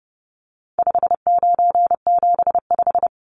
Morse code, heard as a keyed tone, about 30 wpm, 700 Hz. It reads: 5975
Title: Numbers - 4-Digit 30wpm